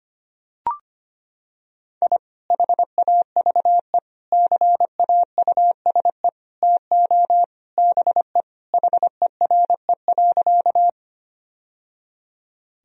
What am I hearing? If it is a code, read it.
Morse code, heard as a keyed tone, about 25 wpm, 700 Hz. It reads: I HAVE CAUSE TO BE HERE.
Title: Sentences from Top 1000 Words - Review Part 1 - 25wpm